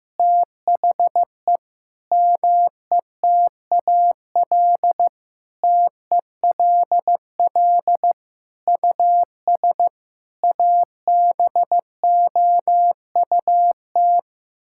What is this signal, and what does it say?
Morse code, heard as a keyed tone, about 15 wpm, 700 Hz. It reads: THE METAL TELL US ABOUT
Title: Sentences from Top 1000 Words - Review Part 6 - 15wpm